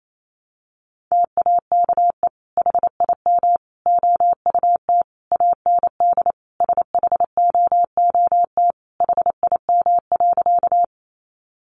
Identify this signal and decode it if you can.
Morse code, heard as a keyed tone, about 28 wpm, 700 Hz. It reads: TAKE HIM OUT AND SHOOT HIM.
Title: Sentences from Top 1200 Words 28wpm